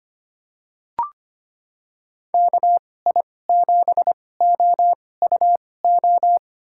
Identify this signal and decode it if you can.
Morse code, heard as a keyed tone, about 25 wpm, spaced as for 18 wpm, 700 Hz. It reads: KI7OUO
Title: USA Call Signs - 2x Character Spacing - 25wpm